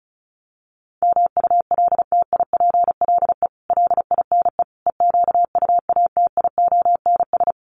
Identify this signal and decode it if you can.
Morse code, heard as a keyed tone, about 34 wpm, 700 Hz. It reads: MULTIPLE LINE EQUATIONS
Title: Sets of 5 Words 35wpm